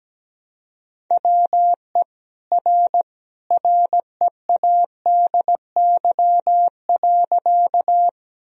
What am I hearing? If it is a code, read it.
Morse code, heard as a keyed tone, about 17 wpm, 700 Hz. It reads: WE R READY.
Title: Sentences from Top 400 Words and Abbreviations 17wpm